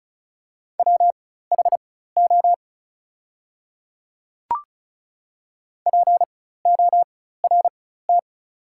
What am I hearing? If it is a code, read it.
Morse code, heard as a keyed tone, about 34 wpm, spaced as for 16 wpm, 690 Hz. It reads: WHO PORT
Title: 4x Character Spacing - Top 200 Words - 35wpm